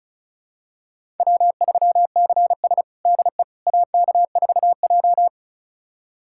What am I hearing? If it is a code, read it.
Morse code, heard as a keyed tone, about 36 wpm, 700 Hz. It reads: W3CS DE AK4J
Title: USA Callsign DE Callsign - 4 Characters - 35wpm